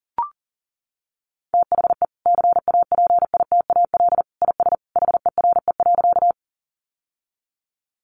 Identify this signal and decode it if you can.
Morse code, heard as a keyed tone, about 40 wpm, 700 Hz. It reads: THE CAPITAL IS HERE.
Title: Sentences from Top 1000 Words 40wpm